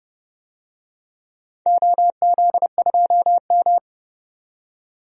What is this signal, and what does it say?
Morse code, heard as a keyed tone, about 31 wpm, 700 Hz. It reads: OZ2M
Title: DXCC-100 Call Signs - 4 Characters - 30wpm